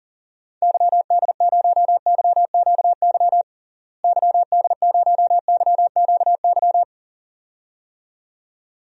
Morse code, heard as a keyed tone, about 40 wpm, 700 Hz. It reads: YD0YQY YD0YQY
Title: DXCC-100 Call Signs - 6 Characters - Repeated 2x - 40wpm